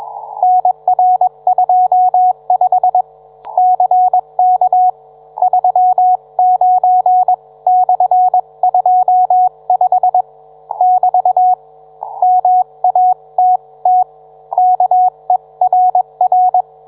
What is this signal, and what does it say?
Morse code, heard as a keyed tone, about 21 wpm, 730 Hz. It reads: NR25 CK 39/25 <BT> MATT KERR
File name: book